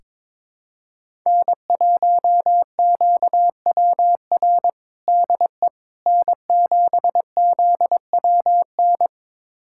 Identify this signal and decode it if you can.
Morse code, heard as a keyed tone, about 22 wpm, 700 Hz. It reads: N1QWR DE N7ZWN
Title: USA Callsign DE Callsign - 5 Characters - 22wpm